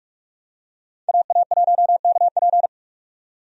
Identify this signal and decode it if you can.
Morse code, heard as a keyed tone, about 46 wpm, 700 Hz. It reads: AA1KP
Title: USA Call Signs - 5 Characters - 45wpm